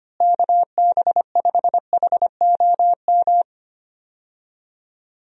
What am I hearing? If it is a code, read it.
Morse code, heard as a keyed tone, about 25 wpm, 700 Hz. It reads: KB5HOM
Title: USA Call Signs - 6 Characters - Repeated 3x - 25wpm